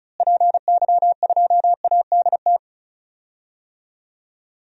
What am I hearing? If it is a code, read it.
Morse code, heard as a keyed tone, about 36 wpm, 700 Hz. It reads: PY2ADT
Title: DXCC-100 Call Signs - 6 Characters - Repeated 2x - 35wpm